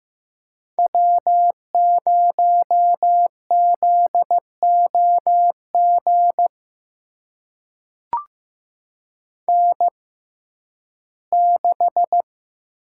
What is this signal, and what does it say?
Morse code, heard as a keyed tone, about 15 wpm, spaced as for 11 wpm, 700 Hz. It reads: W0ZOG N6
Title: USA Call Signs - 5 Characters - 6x Character Spacing - 15wpm